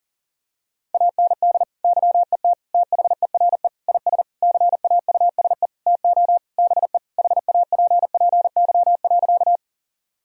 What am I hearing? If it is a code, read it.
Morse code, heard as a keyed tone, about 40 wpm, 700 Hz. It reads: AND YET THERE IS CAUSE TO BE HAPPY.
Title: Sentences from Top 1000 Words - Review Part 1 - 40wpm